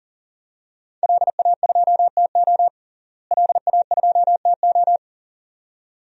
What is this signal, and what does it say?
Morse code, heard as a keyed tone, about 37 wpm, 700 Hz. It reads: LA2TO LA2TO
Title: DXCC-100 Call Signs - 5 Characters - Repeated 2x - 40wpm